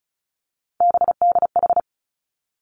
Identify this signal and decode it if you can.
Morse code, heard as a keyed tone, about 36 wpm, 700 Hz. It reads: BDH
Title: Three Challenging Letter-Number Combinations 35wpm